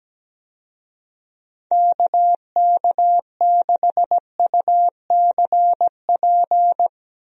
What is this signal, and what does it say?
Morse code, heard as a keyed tone, about 17 wpm, 700 Hz. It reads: KK6UCP
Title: USA Call Signs - 6 Characters - Repeated 2x - 17wpm